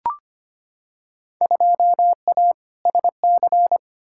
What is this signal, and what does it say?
Morse code, heard as a keyed tone, about 25 wpm, 700 Hz. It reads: 2A SC
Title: Field Day Exchanges - 25wpm